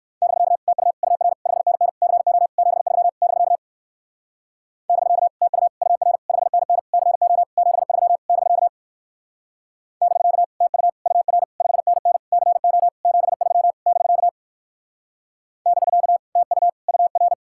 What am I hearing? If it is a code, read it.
Morse code, heard as a keyed tone, about 56 wpm, 700 Hz. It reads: <BK> TU UR 5NN CO 73 <BK> <BK> TU UR 5NN CO 73 <BK> <BK> TU UR 5NN CO 73 <BK> <BK> TU UR